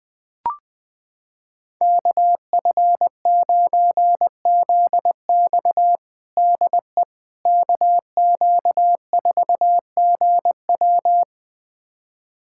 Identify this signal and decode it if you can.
Morse code, heard as a keyed tone, about 20 wpm, 700 Hz. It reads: KF9ZX DE KQ4GW
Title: USA Callsign DE Callsign - 5 Characters - 20wpm